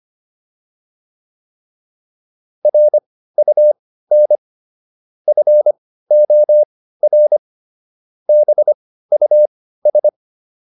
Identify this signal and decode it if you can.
Morse code, heard as a keyed tone, about 25 wpm, spaced as for 15 wpm, 600 Hz. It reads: RUN FOR BUS